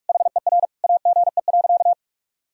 Morse code, heard as a keyed tone, about 46 wpm, 700 Hz. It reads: HER AGE.
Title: Sentences from Top 1000 Words - Review Part 3 - 45wpm